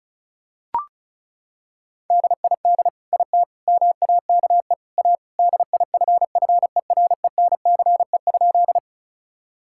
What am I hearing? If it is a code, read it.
Morse code, heard as a keyed tone, about 36 wpm, 700 Hz. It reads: DID IT MAKE A DIFFERENCE?
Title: Sentences from Top 1000 Words - Review Part 5 - 35wpm